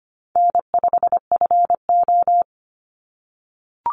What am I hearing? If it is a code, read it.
Morse code, heard as a keyed tone, about 25 wpm, 700 Hz. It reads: N5FO
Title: Easy Call Signs - 25wpm